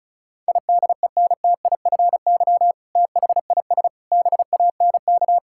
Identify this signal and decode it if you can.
Morse code, heard as a keyed tone, about 36 wpm, 700 Hz. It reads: IDENTIFY THIS BANK
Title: Sentences from Top 1000 Words - Review Part 6 - 35wpm